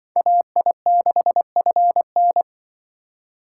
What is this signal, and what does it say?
Morse code, heard as a keyed tone, about 24 wpm, 700 Hz. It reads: AI6FN
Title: Speed Racing - USA 2x2 Call Signs 20wpm